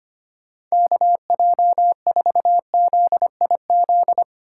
Morse code, heard as a keyed tone, about 32 wpm, 700 Hz. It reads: KJ4ZIZ